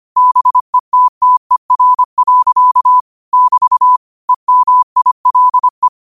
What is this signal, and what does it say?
Morse code, heard as a keyed tone, about 25 wpm, 1000 Hz. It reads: BETTER. <BT> EMILE
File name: qotdfiles / 25wpm